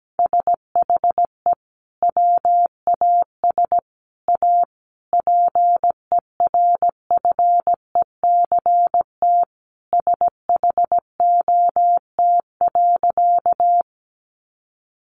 Morse code, heard as a keyed tone, about 17 wpm, 700 Hz. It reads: SHE WAS A PERFECT SHOT.
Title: Sentences from Top 1100 Words 17wpm